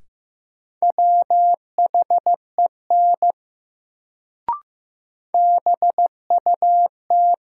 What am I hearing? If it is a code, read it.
Morse code, heard as a keyed tone, about 15 wpm, 700 Hz. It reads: WHEN BUT
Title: Sentences from Top 100 Words - New Word Drill - 15wpm